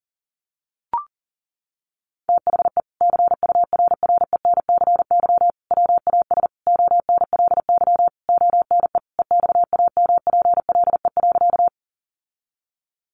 Morse code, heard as a keyed tone, about 37 wpm, 700 Hz. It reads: THE CURRENCY WAS ONLY ONE EXAMPLE.
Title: Sentences from Top 1500 Words 40wpm